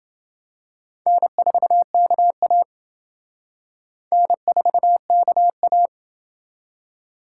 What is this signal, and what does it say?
Morse code, heard as a keyed tone, about 28 wpm, 700 Hz. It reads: N4KA N4KA